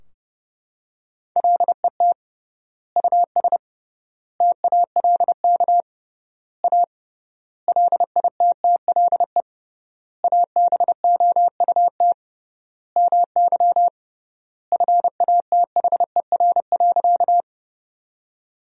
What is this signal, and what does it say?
Morse code, heard as a keyed tone, about 30 wpm, 700 Hz. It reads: LET US TALK A LITTLE ABOUT MY FATHER.
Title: 3x Word-Spacing - Sentences from Top 400 Words - 30wpm